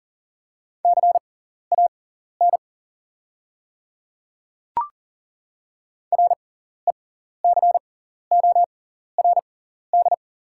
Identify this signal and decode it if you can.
Morse code, heard as a keyed tone, about 40 wpm, spaced as for 14 wpm, 690 Hz. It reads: CAN RECORD
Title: 6x Character Spacing - Top 500 Words - 40wpm